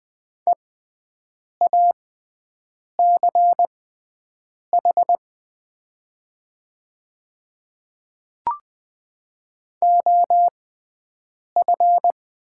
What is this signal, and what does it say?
Morse code, heard as a keyed tone, about 20 wpm, spaced as for 7 wpm, 700 Hz. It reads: EACH OF
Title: 6x Character Spacing - Top 100 Words - 20wpm